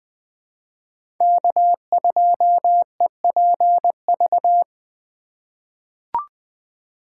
Morse code, heard as a keyed tone, about 20 wpm, 700 Hz. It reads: K2EPV
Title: USA Call Signs - 5 Characters - 4x Character Spacing - 20wpm